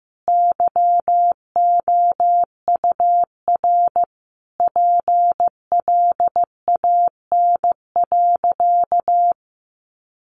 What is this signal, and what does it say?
Morse code, heard as a keyed tone, about 15 wpm, 700 Hz. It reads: YOUR PLAN.
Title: Sentences from Top 500 Words 15wpm